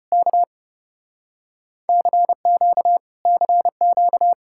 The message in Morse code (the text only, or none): K CQ CQ